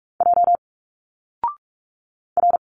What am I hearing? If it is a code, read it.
Morse code, heard as a keyed tone, about 44 wpm, 700 Hz. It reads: J R
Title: Single Letters 45wpm